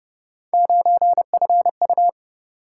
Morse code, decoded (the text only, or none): 9FU